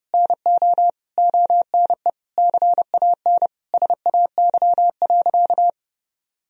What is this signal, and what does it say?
Morse code, heard as a keyed tone, about 30 wpm, 700 Hz. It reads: NO ONE CAN SAY.
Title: Sentences from Top 1000 Words - Review Part 2 - 30wpm